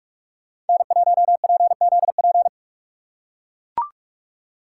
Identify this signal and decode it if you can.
Morse code, heard as a keyed tone, about 42 wpm, 700 Hz. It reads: N1PZP
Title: USA Call Signs - 5 Characters - 6x Character Spacing - 45wpm